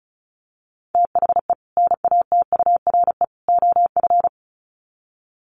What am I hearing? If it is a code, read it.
Morse code, heard as a keyed tone, about 34 wpm, 700 Hz. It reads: THE NATURE OF
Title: Sets of 3 Words 35wpm